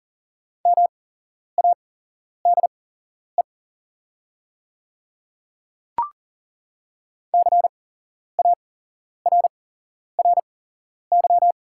Morse code, heard as a keyed tone, about 40 wpm, spaced as for 11 wpm, 690 Hz. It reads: MADE CARRY
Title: 8x Character Spacing - Top 300 Words - 40wpm